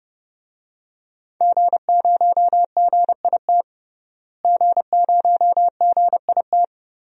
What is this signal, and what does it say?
Morse code, heard as a keyed tone, about 30 wpm, 700 Hz. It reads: G0GIT G0GIT
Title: DXCC-100 Call Signs - 5 Characters - Repeated 2x - 30wpm